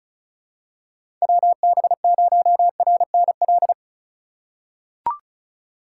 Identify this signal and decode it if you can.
Morse code, heard as a keyed tone, about 36 wpm, 700 Hz. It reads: WB0RNL
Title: USA Call Signs - 6 Characters - 2x Character Spacing - 35wpm